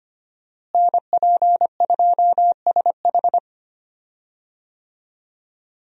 Morse code, heard as a keyed tone, about 25 wpm, 700 Hz. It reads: NP2SH
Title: USA Call Signs - 5 Characters - Repeated 2x - 25wpm